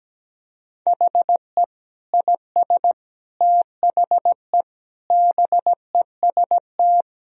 Morse code, heard as a keyed tone, about 17 wpm, 700 Hz. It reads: HE IS THE BEST
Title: Sentences from Top 1000 Words - Review Part 4 - 17wpm